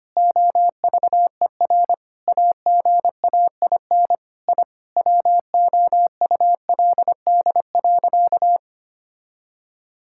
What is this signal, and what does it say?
Morse code, heard as a keyed tone, about 25 wpm, 700 Hz. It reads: OVER AGAIN I WOULD.
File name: practice / 25wpm / Sentences with 9-Word Phrases - 25wpm